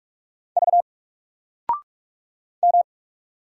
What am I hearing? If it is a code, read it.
Morse code, heard as a keyed tone, about 44 wpm, 700 Hz. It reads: V M
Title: Single Letter-Number 45wpm